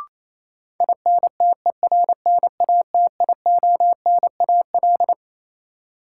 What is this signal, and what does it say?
Morse code, heard as a keyed tone, about 28 wpm, 700 Hz. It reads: INTERNATIONAL
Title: Sentences from Top 1000 Words - Review Part 4 - New Word Drill - 28wpm